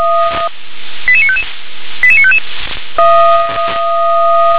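Five-tone selective call (ZVEI), heard as quick runs of five tones, tones of about 70 ms. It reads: 88955 88955
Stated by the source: ZVEI five-tone selective call received with an SDR.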